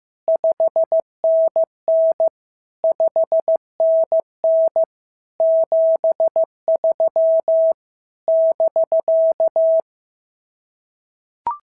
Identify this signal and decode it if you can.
Morse code, heard as a keyed tone, about 14 wpm, 650 Hz. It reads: 5NN 5NN 73 <BK>
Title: SOTA QSO Practice 15wpm